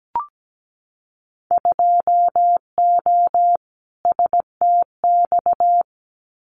Morse code, heard as a keyed tone, about 17 wpm, 700 Hz. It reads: 2O STX
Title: Winter Field Day Exchanges 17wpm